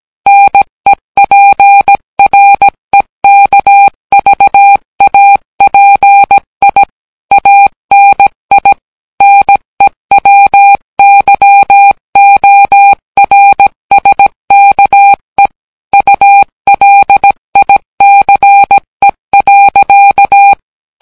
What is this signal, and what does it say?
Morse code, heard as a keyed tone, about 17 wpm, 800 Hz. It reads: NEPREKVAPI ANI NEWYORSKE ULICE.